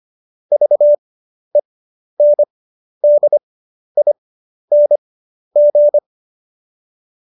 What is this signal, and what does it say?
Morse code, heard as a keyed tone, about 25 wpm, spaced as for 11 wpm, 590 Hz. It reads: VENDING
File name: Jokes-25-char-10-eff